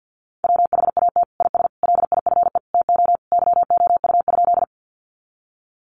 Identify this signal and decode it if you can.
Morse code, heard as a keyed tone, about 50 wpm, 700 Hz. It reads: WHAT IS LIFE TO YOU?
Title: Sentences from Top 300 Words 50wpm